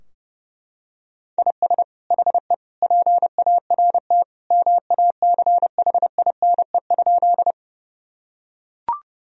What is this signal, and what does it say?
Morse code, heard as a keyed tone, about 32 wpm, 700 Hz. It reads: IS HE PART MACHINE?
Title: Sentences from Top 1000 Words - Review Part 2 - 30wpm